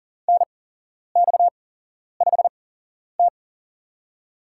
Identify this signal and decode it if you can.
Morse code, heard as a keyed tone, about 40 wpm, spaced as for 11 wpm, 700 Hz. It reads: NX5T
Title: USA Call Signs - 8x Character Spacing - 40wpm